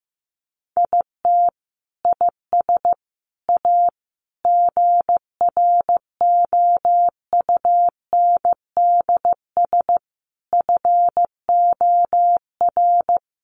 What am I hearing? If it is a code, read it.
Morse code, heard as a keyed tone, about 15 wpm, 700 Hz. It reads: IT IS A GROUNDS FOR
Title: Sentences from Top 1300 Words 15wpm